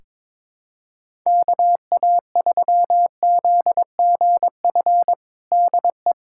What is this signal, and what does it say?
Morse code, heard as a keyed tone, about 22 wpm, 700 Hz. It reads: KA3ZGF DE
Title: USA Callsign DE Callsign 22wpm